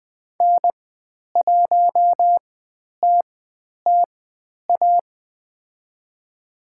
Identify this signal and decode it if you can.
Morse code, heard as a keyed tone, about 20 wpm, spaced as for 10 wpm, 700 Hz. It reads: N1TTA
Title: 095 Lesson 16g - Letter c - Call Signs 20wpm